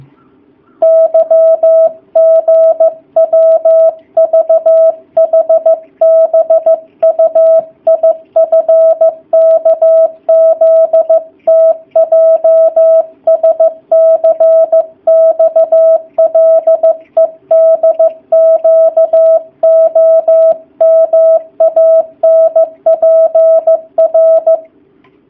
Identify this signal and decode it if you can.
Morse code, heard as a keyed tone, about 15 wpm, 650 Hz. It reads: YGWVHBUIFKZTJSCXLEDQOMANPR
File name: jh8cbh / 2019cw / sp15- w